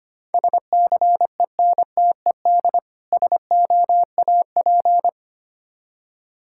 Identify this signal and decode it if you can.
Morse code, heard as a keyed tone, about 25 wpm, 700 Hz. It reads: SCENTED SOAP
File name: practice / 25wpm / Difficult Words with Hints - 25wpm